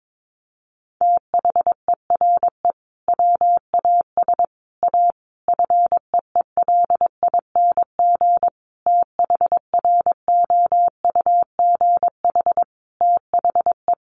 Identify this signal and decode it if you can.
Morse code, heard as a keyed tone, about 22 wpm, 700 Hz. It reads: THERE WAS A FEELING THROUGH THE
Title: Sentences from Top 1200 Words 22wpm